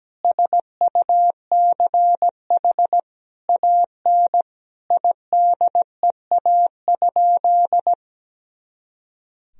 Morse code, heard as a keyed tone, about 17 wpm, 700 Hz. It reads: SUCH AN IDEA?
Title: Sentences from Top 400 Words and Abbreviations 17wpm